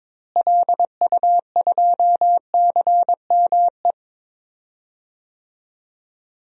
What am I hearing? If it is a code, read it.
Morse code, heard as a keyed tone, about 22 wpm, 700 Hz. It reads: LU2CME
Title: DXCC-100 Call Signs - 6 Characters - Repeated 3x - 22wpm